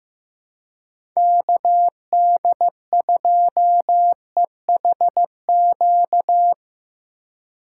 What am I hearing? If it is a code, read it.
Morse code, heard as a keyed tone, about 16 wpm, 700 Hz. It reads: KD2EHQ
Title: USA Call Signs - 6 Characters - 2x Character Spacing - 15wpm